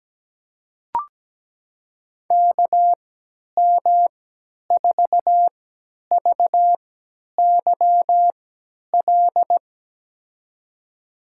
Morse code, heard as a keyed tone, about 17 wpm, spaced as for 10 wpm, 700 Hz. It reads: KM4VYL
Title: USA Call Signs - 6 Characters - 3x Character Spacing - 17wpm